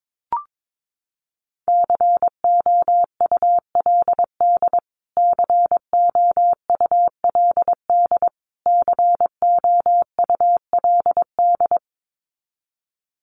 Morse code, heard as a keyed tone, about 22 wpm, 700 Hz. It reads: COULD COULD COULD
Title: Top 200 Words Repeated 22wpm